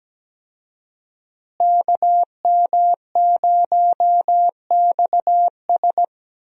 Morse code, heard as a keyed tone, about 17 wpm, 700 Hz. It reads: KM0XS
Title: USA Call Signs - 5 Characters - Repeated 2x - 17wpm